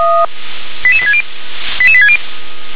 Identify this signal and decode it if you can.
Five-tone selective call (ZVEI), heard as quick runs of five tones, tones of about 70 ms. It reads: 88860 88860